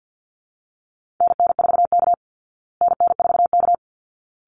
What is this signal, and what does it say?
Morse code, heard as a keyed tone, about 50 wpm, 700 Hz. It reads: NN4K NN4K
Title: Call Signs Repeated 3x - 50wpm